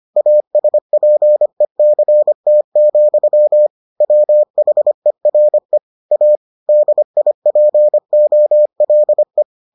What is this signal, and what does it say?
Morse code, heard as a keyed tone, about 25 wpm, 600 Hz. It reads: ASPECT, WHERE A DIPOLE